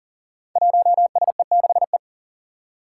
Morse code, heard as a keyed tone, about 41 wpm, 700 Hz. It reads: 1SE6E
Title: Five Challenging Letter-Number Combinations 40wpm